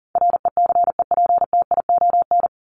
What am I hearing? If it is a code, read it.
Morse code, heard as a keyed tone, about 40 wpm, 700 Hz. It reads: RECEPTION
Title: Top 1751-2000 Words 40wpm